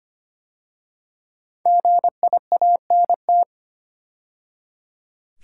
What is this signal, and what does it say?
Morse code, heard as a keyed tone, about 25 wpm, 700 Hz. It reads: GIANT
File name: 5 Letter Words - First 3 Letters Given - 25wpm